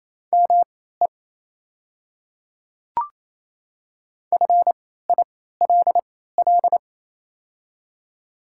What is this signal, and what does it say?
Morse code, heard as a keyed tone, about 28 wpm, spaced as for 16 wpm, 700 Hz. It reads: ME FILL
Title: 3x Character Spacing - Top 500 Words - 28wpm